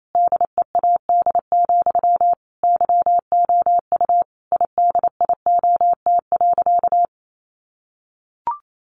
Morse code, heard as a keyed tone, about 28 wpm, 700 Hz. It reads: DEAD, YOU IDIOT.
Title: A Sound Of Thunder 28wpm